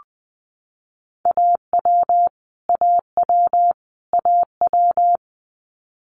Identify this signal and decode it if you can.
Morse code, heard as a keyed tone, about 20 wpm, 700 Hz. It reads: AW AW AW
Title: Multi-letter Phonemes Repeated 3x 20wpm